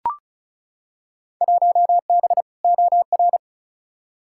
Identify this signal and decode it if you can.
Morse code, heard as a keyed tone, about 34 wpm, 700 Hz. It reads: 1B OR
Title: Field Day Exchanges - 35wpm